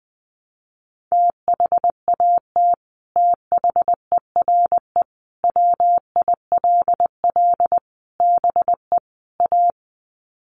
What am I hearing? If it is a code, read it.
Morse code, heard as a keyed tone, about 20 wpm, 700 Hz. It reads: THAT THERE WILL BE A
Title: Sets of 5 Words Encore2 20wpm